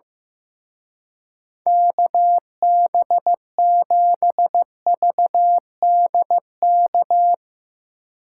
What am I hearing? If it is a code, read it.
Morse code, heard as a keyed tone, about 15 wpm, 700 Hz. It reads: KB7VDK
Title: Speed Racing - USA 2x3 Call Signs 15wpm